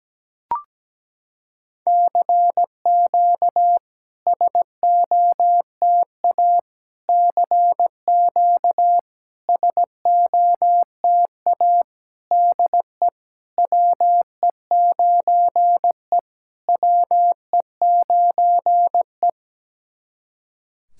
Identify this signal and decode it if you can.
Morse code, heard as a keyed tone, about 17 wpm, 700 Hz. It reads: CQ SOTA CQ SOTA DE WE9E WE9E
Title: SOTA QSO Practice 17wpm